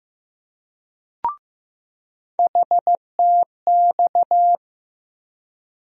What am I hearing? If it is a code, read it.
Morse code, heard as a keyed tone, about 15 wpm, 700 Hz. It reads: HTX
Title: USA Call Sign Postfixes - 3 Letters 15wpm